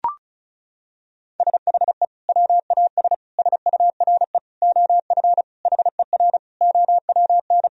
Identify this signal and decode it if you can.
Morse code, heard as a keyed tone, about 36 wpm, 700 Hz. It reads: SHE WAS SURE OF HER OWN
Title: Sentences from Top 1000 Words - Review Part 1 - 35wpm